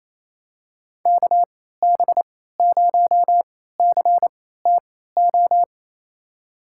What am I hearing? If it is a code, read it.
Morse code, heard as a keyed tone, about 28 wpm, spaced as for 16 wpm, 700 Hz. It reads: KB0CTO
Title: USA Call Signs - 6 Characters - 3x Character Spacing - 28wpm